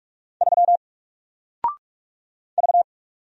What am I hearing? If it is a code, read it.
Morse code, heard as a keyed tone, about 44 wpm, 700 Hz. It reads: 3 V
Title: Single Letter-Number 45wpm